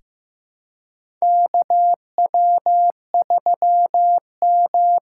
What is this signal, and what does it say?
Morse code, heard as a keyed tone, about 15 wpm, 700 Hz. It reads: KW3M
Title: USA Call Signs - 4 Characters - 3x Character Spacing - 15wpm